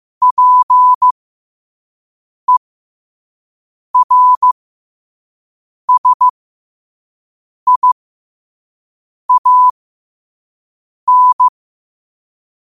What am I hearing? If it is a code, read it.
Morse code, heard as a keyed tone, about 15 wpm, spaced as for 5 wpm, 1000 Hz. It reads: PERSIAN
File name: qotdfiles / 5wpm